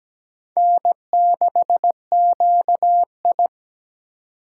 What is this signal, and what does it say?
Morse code, heard as a keyed tone, about 17 wpm, 700 Hz. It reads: N6QI
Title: USA Call Signs - 4 Characters - 4x Character Spacing - 17wpm